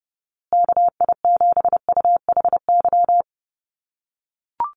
Morse code, heard as a keyed tone, about 30 wpm, 700 Hz. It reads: KI7UHY
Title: USA Call Signs Repeated 2x 30wpm